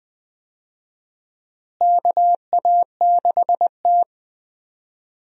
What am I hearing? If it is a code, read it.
Morse code, heard as a keyed tone, about 20 wpm, 700 Hz. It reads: KA6T
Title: USA Call Signs - 4 Characters - Repeated 2x - 20wpm